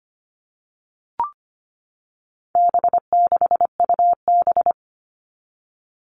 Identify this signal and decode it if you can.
Morse code, heard as a keyed tone, about 25 wpm, 700 Hz. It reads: B6UB
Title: All Cut Numbers - 4-Digit 25wpm